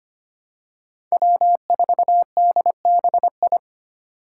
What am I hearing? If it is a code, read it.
Morse code, heard as a keyed tone, about 25 wpm, 700 Hz. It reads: W4DBI
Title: Five Challenging Letter-Number Combinations 25wpm